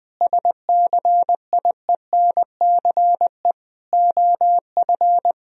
Morse code, heard as a keyed tone, about 20 wpm, 700 Hz. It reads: SCIENCE OF
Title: Sentences from Top 1000 Words - Review Part 2 - 20wpm